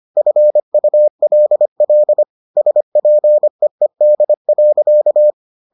Morse code, heard as a keyed tone, about 25 wpm, 600 Hz. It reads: FULL SPEED.